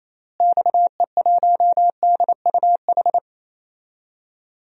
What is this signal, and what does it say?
Morse code, heard as a keyed tone, about 28 wpm, 700 Hz. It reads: XE1DUH
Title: DXCC-100 Call Signs - 6 Characters - Repeated 2x - 28wpm